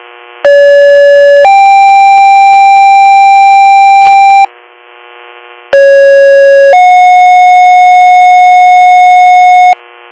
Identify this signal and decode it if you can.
Two-tone page, heard as a one-second tone, then a three-second tone, about 570 then 790 Hz, about 550 then 730 Hz. It is heard from 0.5 to 9.7 s.